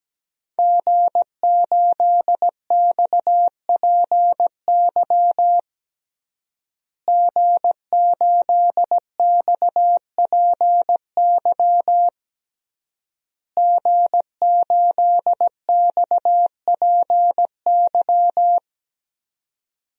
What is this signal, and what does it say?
Morse code, heard as a keyed tone, about 17 wpm, 700 Hz. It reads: G8XPY G8XPY G8XPY
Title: DXCC-100 Call Signs - 5 Characters - Repeated 3x - 17wpm